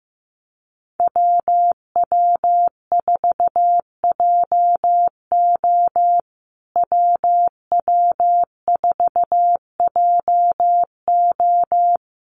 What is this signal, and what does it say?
Morse code, heard as a keyed tone, about 15 wpm, 700 Hz. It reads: WW4JO WW4JO
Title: USA Call Signs Repeated 2x 15wpm